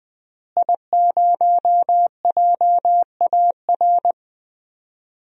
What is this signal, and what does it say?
Morse code, heard as a keyed tone, about 19 wpm, 700 Hz. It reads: I0JAR
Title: DXCC-100 Call Signs - 5 Characters - Repeated 3x - 20wpm